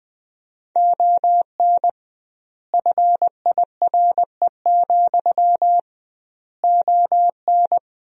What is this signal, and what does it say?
Morse code, heard as a keyed tone, about 20 wpm, 700 Hz. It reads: ON FIRE, ON
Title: 2x Word-Spacing - Sentences from Top 400 Words - 20wpm